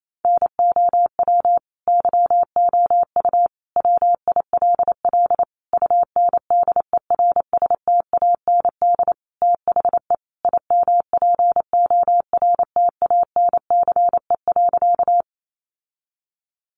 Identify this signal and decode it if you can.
Morse code, heard as a keyed tone, about 28 wpm, 700 Hz. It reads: NOW YOU WILL UNDERSTAND THE IMPORTANCE.
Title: Sentences from Top 1200 Words 28wpm